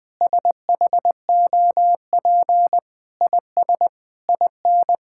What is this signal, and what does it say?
Morse code, heard as a keyed tone, about 20 wpm, 700 Hz. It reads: SHOP IS IN
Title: Sentences with 4-Word Phrases - 20wpm